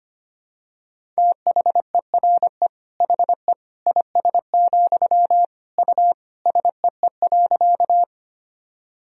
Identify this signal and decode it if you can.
Morse code, heard as a keyed tone, about 25 wpm, 700 Hz. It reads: THERE HE IS, U SEE.
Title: Sentences from Top 100 Words and Abbreviations 25wpm